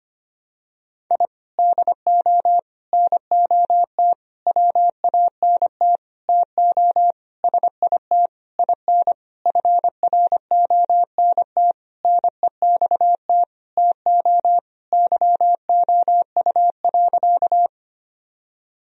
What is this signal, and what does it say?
Morse code, heard as a keyed tone, about 25 wpm, 700 Hz. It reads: I DO NOT WANT TO SIT IN FRONT NEXT TO YOU.
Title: Sentences from Top 1000 Words - Review Part 2 - 25wpm